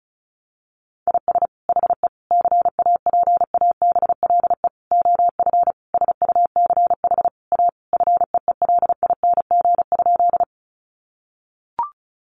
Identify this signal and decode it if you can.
Morse code, heard as a keyed tone, about 34 wpm, 700 Hz. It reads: IS HE CAPABLE OF SUCH A FEELING?
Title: Sentences from Top 1200 Words 35wpm